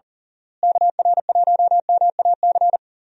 Morse code, heard as a keyed tone, about 37 wpm, 700 Hz. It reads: KR1MAC
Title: USA Call Signs - 6 Characters - Repeated 3x - 40wpm